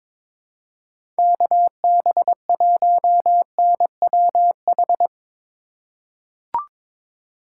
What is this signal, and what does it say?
Morse code, heard as a keyed tone, about 22 wpm, 700 Hz. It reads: KB1NWH
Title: USA Call Signs - 4x Character Spacing - 22wpm